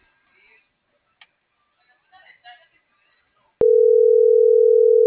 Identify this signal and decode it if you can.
Ringback tone (North America), 440 + 480 Hz.